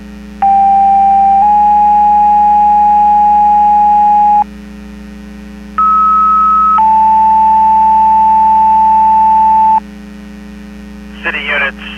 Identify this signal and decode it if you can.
Two-tone page, heard as a one-second tone, then a three-second tone, about 790 then 830 Hz, about 1290 then 880 Hz. It is heard from 0.4 to 9.8 s.
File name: Portage 154.385 Paging